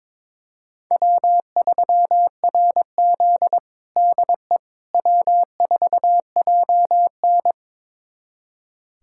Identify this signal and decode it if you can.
Morse code, heard as a keyed tone, about 21 wpm, 700 Hz. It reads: W3RZ DE W4JN
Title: USA Callsign DE Callsign - 4 Characters - 22wpm